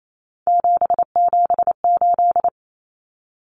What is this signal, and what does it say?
Morse code, heard as a keyed tone, about 28 wpm, 700 Hz. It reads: 778
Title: Common Cut Numbers - 3-Digit 28wpm